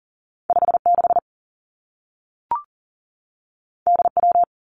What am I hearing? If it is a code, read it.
Morse code, heard as a keyed tone, about 40 wpm, 700 Hz. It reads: 5 6 D W
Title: Two Challenging Letter-Number Combinations 40wpm